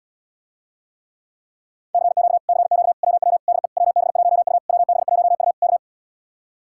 Morse code, heard as a keyed tone, about 80 wpm, 700 Hz. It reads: CQ CQ WW DE WB1U WB1U K